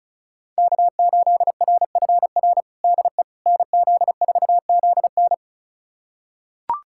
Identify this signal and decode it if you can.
Morse code, heard as a keyed tone, about 36 wpm, 700 Hz. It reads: K8RFR DE NZ4ZN
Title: USA Callsign DE Callsign - 5 Characters - 35wpm